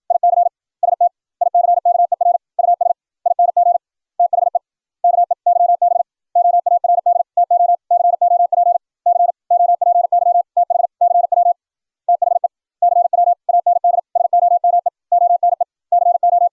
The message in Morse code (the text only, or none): IC STICKER USING THE KEYBOARD TO COP COPY TH CW THE CW AND SOMEONE CO